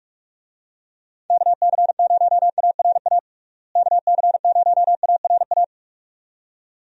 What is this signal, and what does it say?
Morse code, heard as a keyed tone, about 41 wpm, 700 Hz. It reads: KC0ARA KC0ARA
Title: USA Call Signs - 6 Characters - Repeated 2x - 45wpm